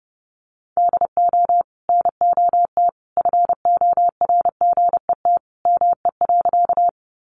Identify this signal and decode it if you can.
Morse code, heard as a keyed tone, about 30 wpm, 700 Hz. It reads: DO NOT FORGET ME.
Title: Sentences from Top 1200 Words 30wpm